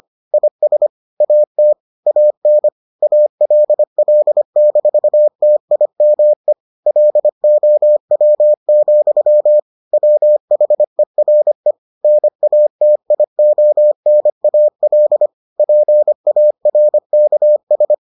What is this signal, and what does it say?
Morse code, heard as a keyed tone, about 25 wpm, 600 Hz. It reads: IS AT AN ALL-TIME LOW, WHERE NATIONAL PARKS